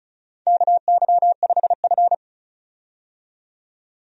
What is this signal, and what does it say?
Morse code, heard as a keyed tone, about 36 wpm, 700 Hz. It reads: KY5F
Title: USA Call Signs - 4 Characters - 35wpm